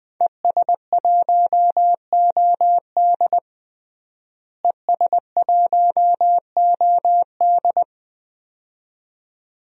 Morse code, heard as a keyed tone, about 20 wpm, 700 Hz. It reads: ES1OD ES1OD
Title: DXCC-100 Call Signs - 5 Characters - Repeated 2x - 20wpm